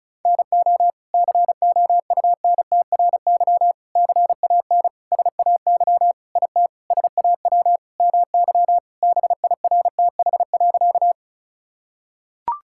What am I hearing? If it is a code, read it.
Morse code, heard as a keyed tone, about 36 wpm, 700 Hz. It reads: NO COUNTRY CAN SAY IT SAW MY BIRTH.
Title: Sentences from Top 1000 Words - Review Part 5 - 35wpm